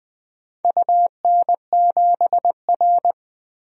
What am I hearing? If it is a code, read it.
Morse code, heard as a keyed tone, about 20 wpm, 700 Hz. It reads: UN7R
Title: DXCC-100 Call Signs - 4 Characters - 20wpm